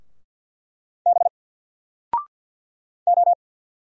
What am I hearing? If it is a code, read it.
Morse code, heard as a keyed tone, about 48 wpm, 700 Hz. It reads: B O
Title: Single Letter-Number Alt 50wpm